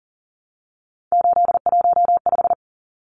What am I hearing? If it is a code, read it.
Morse code, heard as a keyed tone, about 37 wpm, 700 Hz. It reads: 815
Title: Numbers - 3-Digit 40wpm